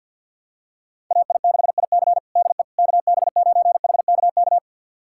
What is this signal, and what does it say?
Morse code, heard as a keyed tone, about 50 wpm, 700 Hz. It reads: AI6IC DE KB9HKK
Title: USA Callsign DE Callsign 50wpm